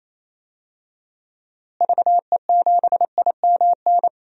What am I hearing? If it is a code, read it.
Morse code, heard as a keyed tone, about 28 wpm, 700 Hz. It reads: VE7IMN
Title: DXCC-100 Call Signs - 6 Characters - Repeated 2x - 28wpm